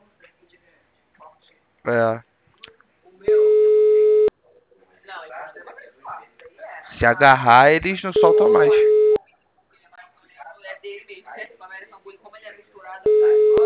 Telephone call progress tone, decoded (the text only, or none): ringback tone